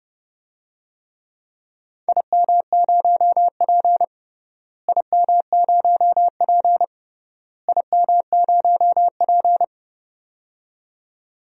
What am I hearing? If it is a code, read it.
Morse code, heard as a keyed tone, about 31 wpm, 700 Hz. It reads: IM0P IM0P IM0P
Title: DXCC-100 Call Signs - 4 Characters - Repeated 3x - 30wpm